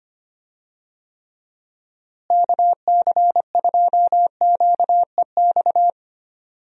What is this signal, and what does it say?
Morse code, heard as a keyed tone, about 25 wpm, 700 Hz. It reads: KC2QEX
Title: USA Call Signs - 6 Characters - Repeated 2x - 25wpm